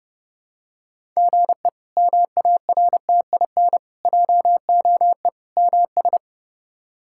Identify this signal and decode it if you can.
Morse code, heard as a keyed tone, about 30 wpm, 700 Hz. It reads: GE MARTIN JOE MS
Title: K1USN SST QSO Practice - 30wpm